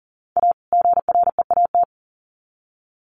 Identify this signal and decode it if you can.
Morse code, heard as a keyed tone, about 40 wpm, 700 Hz. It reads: A GREAT
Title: Sets of 2 Words 40wpm